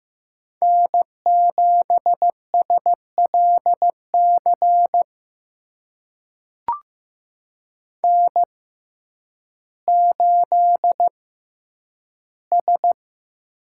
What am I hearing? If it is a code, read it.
Morse code, heard as a keyed tone, about 15 wpm, spaced as for 9 wpm, 700 Hz. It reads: N7SLC N8S
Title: USA Call Signs - 5 Characters - 6x Character Spacing - 15wpm